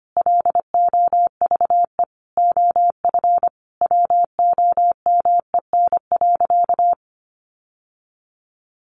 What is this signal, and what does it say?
Morse code, heard as a keyed tone, about 25 wpm, 700 Hz. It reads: LOVE OF WOMEN.
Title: Sentences from Top 1400 Words 25wpm